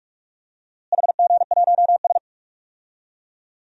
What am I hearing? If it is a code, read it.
Morse code, heard as a keyed tone, about 46 wpm, 700 Hz. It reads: HG1S
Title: DXCC-100 Call Signs - 4 Characters - Repeated 2x - 45wpm